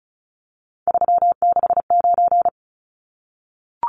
Morse code, heard as a keyed tone, about 34 wpm, 700 Hz. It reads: 369
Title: Numbers - 3-Digit 35wpm